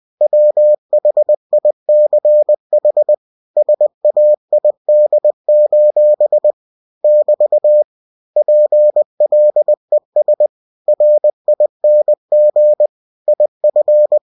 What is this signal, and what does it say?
Morse code, heard as a keyed tone, about 20 wpm, 600 Hz. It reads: WHICH SAID: <BT> PLES RING IF
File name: cw-books / winnie-the-pooh / winnie-20wpm-page-0029